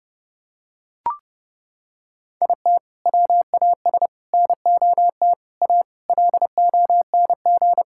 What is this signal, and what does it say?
Morse code, heard as a keyed tone, about 32 wpm, 700 Hz. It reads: IT WAS NOT A LONG
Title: Sentences from Top 1000 Words - Review Part 6 - 30wpm